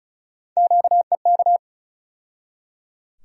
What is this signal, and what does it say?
Morse code, heard as a keyed tone, about 34 wpm, 700 Hz. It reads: QEK
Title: USA Call Sign Postfixes - 3 Letters 35wpm